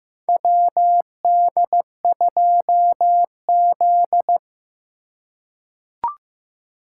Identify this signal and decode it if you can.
Morse code, heard as a keyed tone, about 15 wpm, 700 Hz. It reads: WD2Z
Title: USA Call Signs - 4 Characters - 4x Character Spacing - 15wpm